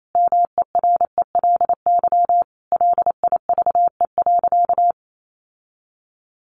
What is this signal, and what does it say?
Morse code, heard as a keyed tone, about 28 wpm, 700 Hz. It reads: MERELY LIVE.
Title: Sentences from Top 1400 Words 28wpm